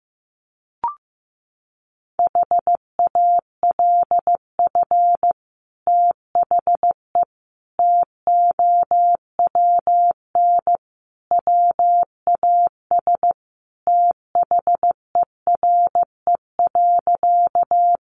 Morse code, heard as a keyed tone, about 15 wpm, 700 Hz. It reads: HALF THE TOWN WAS THERE.
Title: Sentences from Top 500 Words 15wpm